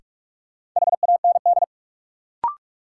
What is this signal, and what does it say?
Morse code, heard as a keyed tone, about 46 wpm, 700 Hz. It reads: HAND
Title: Sets of 3 Words - New Word Drill - 45wpm